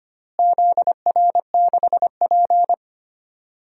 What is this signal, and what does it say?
Morse code, heard as a keyed tone, about 25 wpm, 700 Hz. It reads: ZR6P
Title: DXCC-100 Call Signs - 4 Characters - Repeated 3x - 25wpm